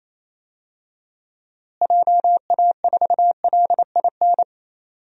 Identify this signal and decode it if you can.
Morse code, heard as a keyed tone, about 28 wpm, 700 Hz. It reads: JA4LIN
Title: DXCC-100 Call Signs - 6 Characters - Repeated 2x - 28wpm